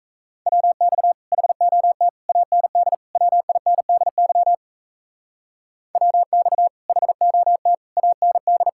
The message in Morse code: WX HOT AND WINDY WX HOT AND

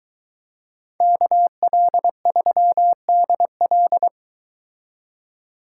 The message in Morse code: KL3DL

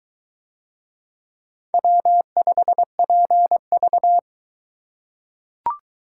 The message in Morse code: W5PV

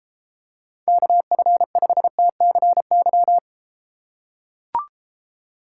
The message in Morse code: KF5TCY

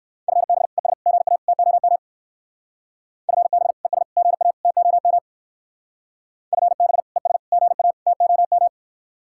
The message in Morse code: FB ES GA TOM FB ES GA TOM FB ES GA TOM

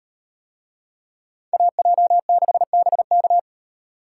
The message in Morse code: AJ6BK

